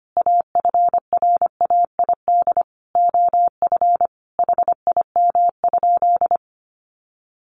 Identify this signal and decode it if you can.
Morse code, heard as a keyed tone, about 25 wpm, 700 Hz. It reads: AFRAID OF HIM?
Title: Sentences from Top 1500 Words 25wpm